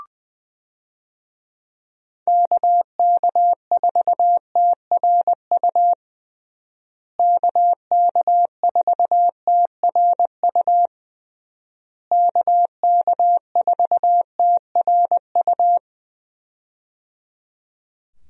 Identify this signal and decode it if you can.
Morse code, heard as a keyed tone, about 20 wpm, 700 Hz. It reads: KK4TRU KK4TRU KK4TRU
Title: USA Call Signs - 6 Characters - Repeated 3x - 20wpm